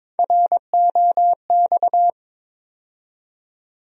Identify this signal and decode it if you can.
Morse code, heard as a keyed tone, about 21 wpm, 700 Hz. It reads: ROX
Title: USA Call Sign Postfixes - 3 Letters 22wpm